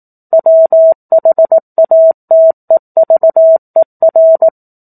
Morse code, heard as a keyed tone, about 18 wpm, 650 Hz. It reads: WHATEVER
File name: 04-Chapter-04 18 WPM